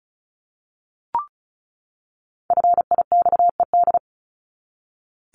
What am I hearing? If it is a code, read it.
Morse code, heard as a keyed tone, about 34 wpm, 700 Hz. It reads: FIXED
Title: Top 1001-1250 Words 35wpm